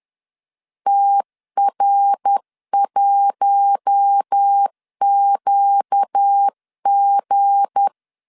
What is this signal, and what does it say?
Morse code, heard as a keyed tone, about 10 wpm, 800 Hz. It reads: TR1QG
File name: LEZIONI / 7 GF9Z / 7 G 10wpm 2501